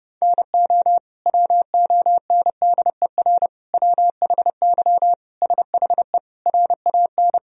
Morse code, heard as a keyed tone, about 32 wpm, 700 Hz. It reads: NO WONDER WHY SHE RAN
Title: Sentences from Top 1000 Words - Review Part 3 - 30wpm